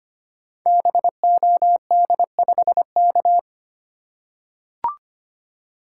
Morse code, heard as a keyed tone, about 25 wpm, 700 Hz. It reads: BOD5K
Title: Code Groups - 5 Letter:Numbers - 25wpm